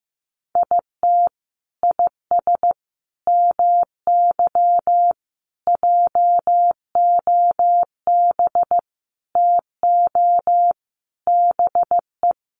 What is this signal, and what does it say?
Morse code, heard as a keyed tone, about 15 wpm, 700 Hz. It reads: IT IS MY JOB TO BE
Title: Sentences from Top 600 Words 15wpm